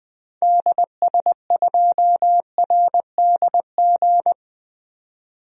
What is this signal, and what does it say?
Morse code, heard as a keyed tone, about 20 wpm, 700 Hz. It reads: DS2RDG
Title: DXCC-100 Call Signs - 6 Characters - 20wpm